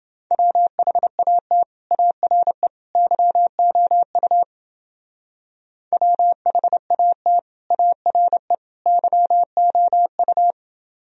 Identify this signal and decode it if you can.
Morse code, heard as a keyed tone, about 29 wpm, 700 Hz. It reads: WHAT ARE YOU WHAT ARE YOU